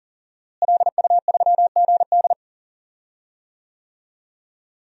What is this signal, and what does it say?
Morse code, heard as a keyed tone, about 40 wpm, 700 Hz. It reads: LU3GD
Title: DXCC-100 Call Signs - 5 Characters - Repeated 2x - 40wpm